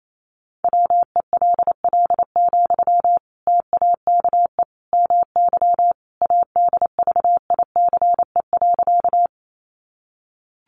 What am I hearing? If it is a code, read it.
Morse code, heard as a keyed tone, about 28 wpm, 700 Hz. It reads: WELL, TAKE MY ADVICE.
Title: Sentences from Top 1100 Words 28wpm